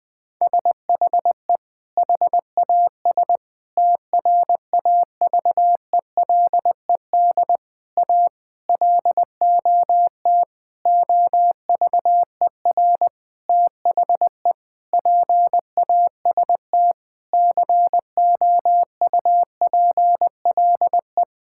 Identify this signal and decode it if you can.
Morse code, heard as a keyed tone, about 19 wpm, 700 Hz. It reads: SHE HAS TRAVELED A LOT OVER THE PAST COUPLE
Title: Sentences with 6-Word Phrases - 20wpm